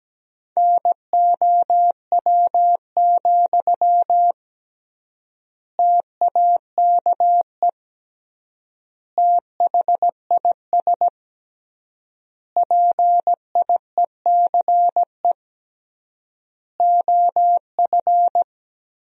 Morse code, heard as a keyed tone, about 17 wpm, 700 Hz. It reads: NOW, TAKE THIS PIECE OF
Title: 3x Word-Spacing - Sentences from Top 400 Words - 17wpm